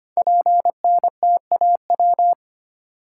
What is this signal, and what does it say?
Morse code, heard as a keyed tone, about 25 wpm, 700 Hz. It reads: PNTAW
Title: Code Groups - 5 Letters - 25wpm